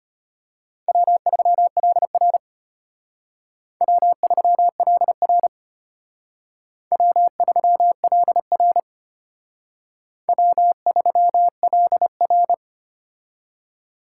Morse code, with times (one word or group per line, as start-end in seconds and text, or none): 0.9-2.4 W3LR
3.8-5.5 W3LR
6.9-8.8 W3LR
10.3-12.5 W3LR